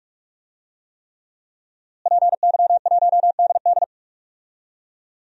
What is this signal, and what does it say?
Morse code, heard as a keyed tone, about 43 wpm, 700 Hz. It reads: PY1DD
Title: DXCC-100 Call Signs - 5 Characters - Repeated 3x - 45wpm